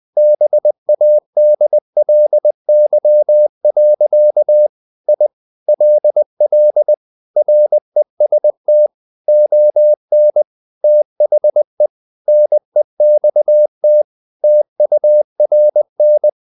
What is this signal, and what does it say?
Morse code, heard as a keyed tone, about 20 wpm, 600 Hz. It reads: BADLY. I LL REST ON THE NEXT TURN
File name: the old man and the sea part 2 20wpm